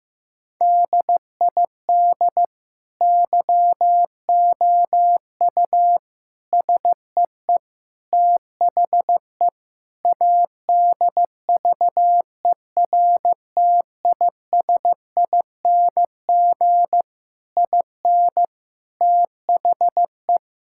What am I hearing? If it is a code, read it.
Morse code, heard as a keyed tone, about 15 wpm, 700 Hz. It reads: DID YOU SEE THE ADVERTISING IN THE
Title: Sentences from Top 1000 Words - Review Part 6 - 15wpm